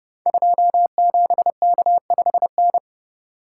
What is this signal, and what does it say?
Morse code, heard as a keyed tone, about 32 wpm, 700 Hz. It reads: 27K5N
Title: Code Groups - 5 Letter:Numbers - 30wpm